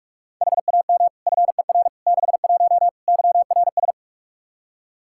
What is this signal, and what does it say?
Morse code, heard as a keyed tone, about 45 wpm, 700 Hz. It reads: HAM FER 61 YRS
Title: Practice QSO - Exchange 3 - More Info - 45wpm